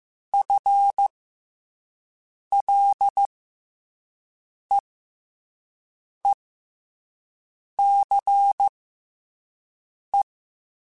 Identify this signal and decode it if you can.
Morse code, heard as a keyed tone, about 15 wpm, spaced as for 5 wpm, 800 Hz. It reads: FLEECE
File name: FL Runs / FL Runs mp3s / FL Runs @5